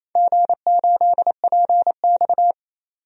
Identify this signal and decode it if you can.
Morse code, heard as a keyed tone, about 27 wpm, 700 Hz. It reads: G8PX
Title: DXCC-100 Call Signs - 4 Characters - 28wpm